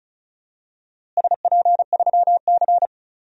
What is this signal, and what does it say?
Morse code, heard as a keyed tone, about 36 wpm, 690 Hz. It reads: SP3C
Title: DXCC-100 Call Signs - 4 Characters - Repeated 3x - 35wpm